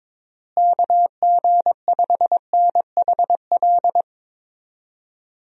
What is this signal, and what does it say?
Morse code, heard as a keyed tone, about 22 wpm, 700 Hz. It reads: KG5NHL
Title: USA Call Signs - 6 Characters - Repeated 2x - 22wpm